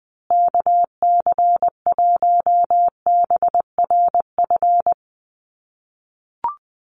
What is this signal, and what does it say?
Morse code, heard as a keyed tone, about 20 wpm, 700 Hz. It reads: KC1BRF
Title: USA Call Signs Repeated 2x 20wpm